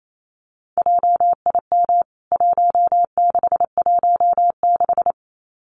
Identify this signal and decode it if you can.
Morse code, heard as a keyed tone, about 28 wpm, 700 Hz. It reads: JIM 1616
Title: CWT Exchange 28wpm